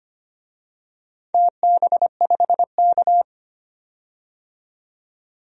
Morse code, heard as a keyed tone, about 25 wpm, 700 Hz. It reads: TB5K
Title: DXCC-100 Call Signs - 4 Characters - 25wpm